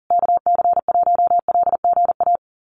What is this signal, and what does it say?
Morse code, heard as a keyed tone, about 40 wpm, 700 Hz. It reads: KC1LGA
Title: USA Call Signs Repeated 3x 40wpm